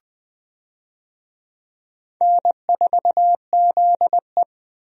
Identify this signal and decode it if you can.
Morse code, heard as a keyed tone, about 21 wpm, 700 Hz. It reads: N4ZE
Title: USA Call Signs - 4 Characters - Repeated 3x - 20wpm